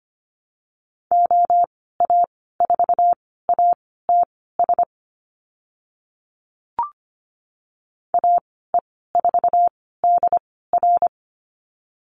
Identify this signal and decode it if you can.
Morse code, heard as a keyed tone, about 25 wpm, spaced as for 16 wpm, 700 Hz. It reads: OA4ATS AE4DR
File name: Lesson 15g - Number 5 - Call Signs 25wpm